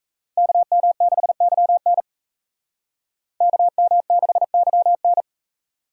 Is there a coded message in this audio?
Yes: Morse code from 0.4 to 5.2 s, heard as a keyed tone, about 36 wpm, 700 Hz. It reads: KM6YN KM6YN